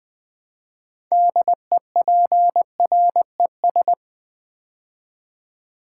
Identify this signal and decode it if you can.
Morse code, heard as a keyed tone, about 20 wpm, 700 Hz. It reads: DEPRES
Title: Build A Word - 3 Letters At A Time - 9-Letter Words - 20wpm